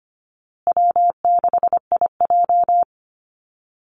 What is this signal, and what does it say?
Morse code, heard as a keyed tone, about 25 wpm, 700 Hz. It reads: W6IJ
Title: Call Signs Repeated 2x - 25wpm